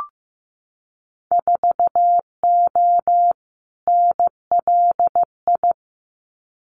Morse code, heard as a keyed tone, about 15 wpm, 700 Hz. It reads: 4O NLI
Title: Winter Field Day Exchanges Encore 15wpm